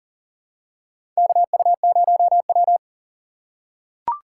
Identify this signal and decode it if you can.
Morse code, heard as a keyed tone, about 40 wpm, 700 Hz. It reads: KU0W
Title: USA Call Signs - 4 Characters - 6x Character Spacing - 40wpm